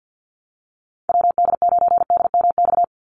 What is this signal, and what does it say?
Morse code, heard as a keyed tone, about 50 wpm, 700 Hz. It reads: WD9DMX
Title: USA Call Signs 50wpm